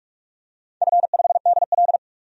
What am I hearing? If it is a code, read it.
Morse code, heard as a keyed tone, about 44 wpm, 700 Hz. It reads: F5DL
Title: DXCC-100 Call Signs - 4 Characters - 45wpm